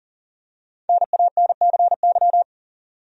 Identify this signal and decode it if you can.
Morse code, heard as a keyed tone, about 40 wpm, 700 Hz. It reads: NANCY
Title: USA Common Names - Max 5 Letters - 40wpm